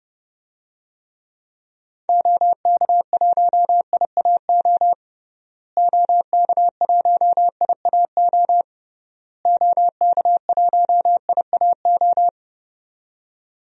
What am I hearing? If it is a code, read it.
Morse code, heard as a keyed tone, about 30 wpm, 700 Hz. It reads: OK1IAO OK1IAO OK1IAO
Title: DXCC-100 Call Signs - 6 Characters - Repeated 3x - 30wpm